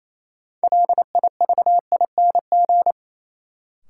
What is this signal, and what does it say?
Morse code, heard as a keyed tone, about 28 wpm, 700 Hz. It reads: LIVING
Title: Sets of 4 Words - New Word Drill - 28wpm